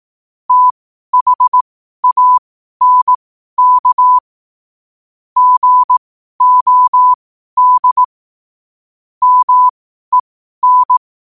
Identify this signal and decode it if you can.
Morse code, heard as a keyed tone, about 18 wpm, spaced as for 12 wpm, 1000 Hz. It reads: THANK GOD MEN
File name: qotdfiles / 12wpm